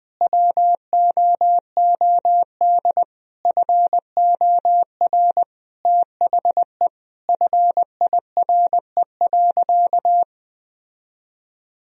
Morse code, heard as a keyed tone, about 20 wpm, 700 Hz. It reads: WOOD FOR THE FIRE.
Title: Sentences from Top 1000 Words - Review Part 2 - 20wpm